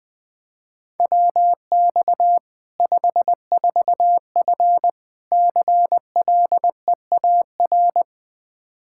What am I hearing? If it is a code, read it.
Morse code, heard as a keyed tone, about 20 wpm, 700 Hz. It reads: WX 54F CLEAR
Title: K1USN SST QSO Practice 20wpm